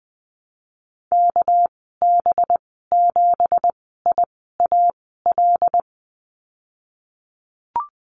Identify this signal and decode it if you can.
Morse code, heard as a keyed tone, about 20 wpm, spaced as for 15 wpm, 700 Hz. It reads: KB7IAL
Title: 200 Lesson 32g - Letter k - Call Signs 20wpm